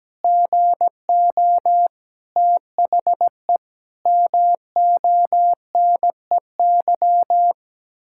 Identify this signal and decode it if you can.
Morse code, heard as a keyed tone, about 17 wpm, 700 Hz. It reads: GO THE MONEY
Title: Pangram Challenge v2 - 17wpm